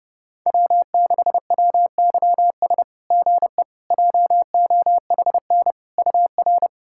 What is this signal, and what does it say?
Morse code, heard as a keyed tone, about 30 wpm, 690 Hz. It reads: W6WYS GE JOHN UR
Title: SOTA QSO Practice 30wpm